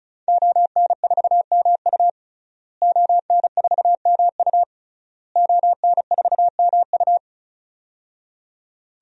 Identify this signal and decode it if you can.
Morse code, heard as a keyed tone, about 36 wpm, 700 Hz. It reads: ON4MU ON4MU ON4MU
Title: DXCC-100 Call Signs - 5 Characters - Repeated 3x - 35wpm